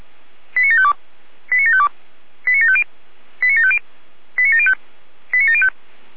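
Five-tone selective call (ZVEI), heard as quick runs of five tones, tones of about 80 ms. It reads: 89752 89752 89750 89750 89705 89705
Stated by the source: ZVEI five-tone call 89584 received by SDR.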